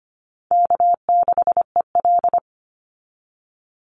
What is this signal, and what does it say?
Morse code, heard as a keyed tone, about 25 wpm, 700 Hz. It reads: K6EL
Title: NA SOTA Chaser Callsigns 25wpm